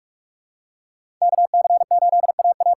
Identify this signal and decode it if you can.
Morse code, heard as a keyed tone, about 44 wpm, 700 Hz. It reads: KC8AA
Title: USA Call Signs - 6 Characters - 8x Character Spacing - 45wpm